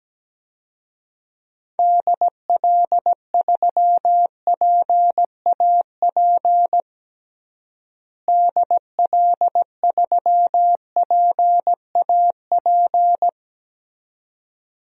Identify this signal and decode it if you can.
Morse code, heard as a keyed tone, about 17 wpm, 700 Hz. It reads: DL3PAP DL3PAP
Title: DXCC-100 Call Signs - 6 Characters - Repeated 2x - 17wpm